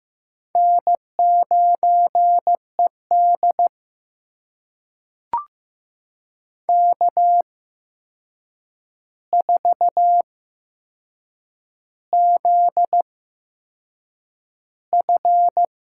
Morse code, heard as a keyed tone, about 15 wpm, spaced as for 6 wpm, 700 Hz. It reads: N9ED K4ZF
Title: USA Call Signs - 4 Characters - 8x Character Spacing - 15wpm